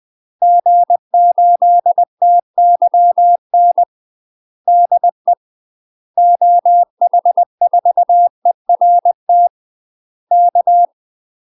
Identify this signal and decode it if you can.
Morse code, heard as a keyed tone, about 20 wpm, 700 Hz. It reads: G8TYN DE OH4ERT K
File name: UK MASTER QSOs TRAINING / 20WPM / UK MASTER QSOs 0072